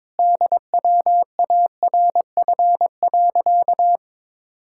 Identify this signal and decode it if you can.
Morse code, heard as a keyed tone, about 22 wpm, 700 Hz. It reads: DWARF.
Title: Pangram Challenge v2 - 22wpm